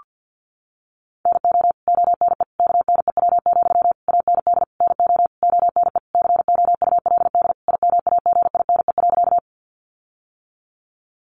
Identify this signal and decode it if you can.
Morse code, heard as a keyed tone, about 50 wpm, 700 Hz. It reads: NO ONE KNEW, AND NO ONE COULD IMAGINE.
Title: Sentences from Top 1400 Words 50wpm